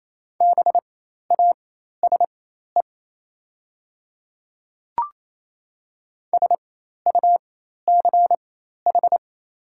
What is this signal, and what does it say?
Morse code, heard as a keyed tone, about 28 wpm, spaced as for 13 wpm, 700 Hz. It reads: BASE SUCH
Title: 4x Character Spacing - Top 500 Words - 28wpm